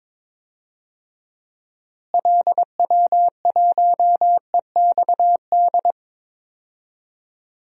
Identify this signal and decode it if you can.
Morse code, heard as a keyed tone, about 22 wpm, 700 Hz. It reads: LW1EXD
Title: DXCC-100 Call Signs - 6 Characters - Repeated 2x - 22wpm